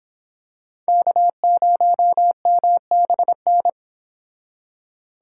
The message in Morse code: K0MBN

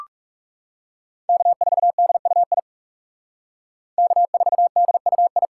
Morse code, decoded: K4DUI K4DUI

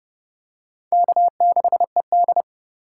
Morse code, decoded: K6ED

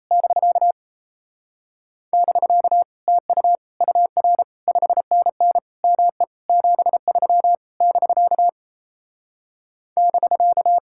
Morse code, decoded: <BK> <BK> TU UR 5NN ME 73 <BK> <BK>

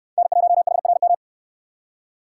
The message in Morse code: N1HRM

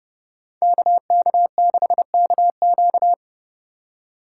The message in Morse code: KK6KQ

